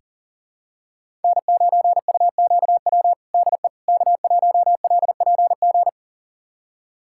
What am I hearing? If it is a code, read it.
Morse code, heard as a keyed tone, about 40 wpm, 700 Hz. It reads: N9UQW DE K1LPG
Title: USA Callsign DE Callsign - 5 Characters - 40wpm